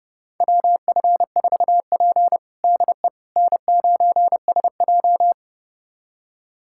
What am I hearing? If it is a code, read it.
Morse code, heard as a keyed tone, about 30 wpm, 700 Hz. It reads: WF4P DE N9SJ
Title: USA Callsign DE Callsign - 4 Characters - 30wpm